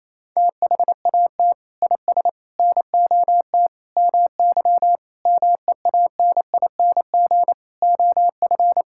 Morse code, heard as a keyed tone, about 28 wpm, 700 Hz. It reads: THAT IS NOT MY MEANING OF
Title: Sentences from Top 1000 Words - Review Part 6 - 28wpm